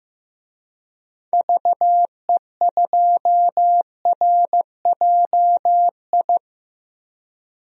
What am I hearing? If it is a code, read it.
Morse code, heard as a keyed tone, about 15 wpm, 700 Hz. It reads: VE2RJI
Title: DXCC-100 Call Signs - 6 Characters - 15wpm